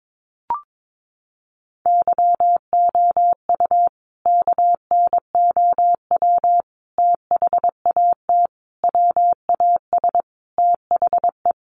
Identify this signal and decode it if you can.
Morse code, heard as a keyed tone, about 22 wpm, 700 Hz. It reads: YOU KNOW THAT WAS THE
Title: Sentences from Top 1500 Words 22wpm